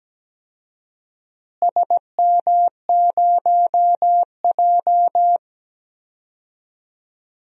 Morse code, heard as a keyed tone, about 17 wpm, 700 Hz. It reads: SM0J
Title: DXCC-100 Call Signs - 4 Characters - Repeated 3x - 17wpm